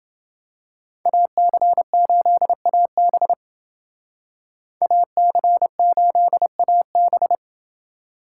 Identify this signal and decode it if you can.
Morse code, heard as a keyed tone, about 28 wpm, 700 Hz. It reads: AC8AB AC8AB